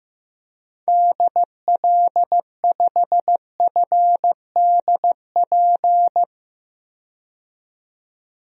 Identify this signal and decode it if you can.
Morse code, heard as a keyed tone, about 15 wpm, 700 Hz. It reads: DL5FDP
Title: DXCC-100 Call Signs - 6 Characters - 15wpm